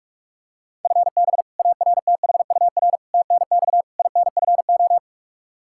Morse code, heard as a keyed tone, about 46 wpm, 700 Hz. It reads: FB ARTHUR TNX INFO
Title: Practice QSO - Exchange 3 - More Info - 45wpm